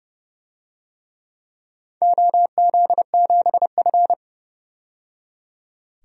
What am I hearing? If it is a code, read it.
Morse code, heard as a keyed tone, about 29 wpm, 700 Hz. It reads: OZ7F
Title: DXCC-100 Call Signs - 4 Characters - 30wpm